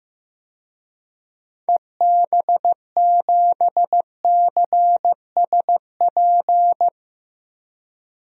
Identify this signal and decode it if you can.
Morse code, heard as a keyed tone, about 16 wpm, 700 Hz. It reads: EB7CSP
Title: DXCC-100 Call Signs - 6 Characters - Repeated 3x - 15wpm